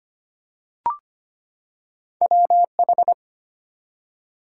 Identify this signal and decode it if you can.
Morse code, heard as a keyed tone, about 25 wpm, 690 Hz. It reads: WH
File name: Two Letter-Number Combinations - 25wpm